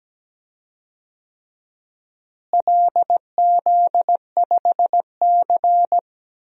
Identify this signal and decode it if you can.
Morse code, heard as a keyed tone, about 17 wpm, 700 Hz. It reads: LZ5C
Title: DXCC-100 Call Signs - 4 Characters - Repeated 3x - 17wpm